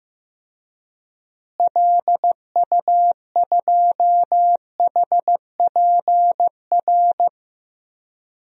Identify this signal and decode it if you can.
Morse code, heard as a keyed tone, about 15 wpm, 700 Hz. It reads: LU2HPR
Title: DXCC-100 Call Signs - 6 Characters - Repeated 2x - 15wpm